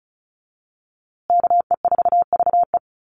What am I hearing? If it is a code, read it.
Morse code, heard as a keyed tone, about 34 wpm, 700 Hz. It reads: KE4VE
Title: USA Call Signs Repeated 3x 35wpm